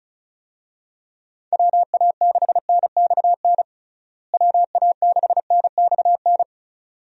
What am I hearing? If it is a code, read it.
Morse code, heard as a keyed tone, about 36 wpm, 700 Hz. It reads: WA6NXN WA6NXN
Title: USA Call Signs - 6 Characters - Repeated 2x - 35wpm